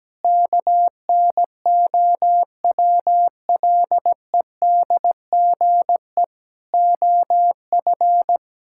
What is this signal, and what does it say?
Morse code, heard as a keyed tone, about 17 wpm, 700 Hz. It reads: KNOWLEDGE OF
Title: Sentences from Top 1000 Words - Review Part 6 - 17wpm